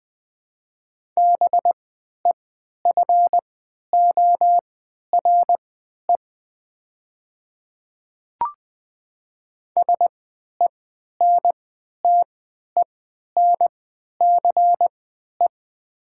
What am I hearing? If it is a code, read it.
Morse code, heard as a keyed tone, about 20 wpm, spaced as for 11 wpm, 700 Hz. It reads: BEFORE SENTENCE
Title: 3x Character Spacing - Top 200 Words - 20wpm